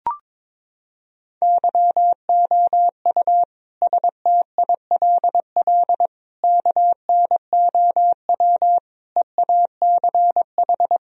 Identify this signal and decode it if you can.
Morse code, heard as a keyed tone, about 22 wpm, 700 Hz. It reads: YOU STILL KNOW EACH
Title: Sentences from Top 1000 Words - Review Part 1 - 22wpm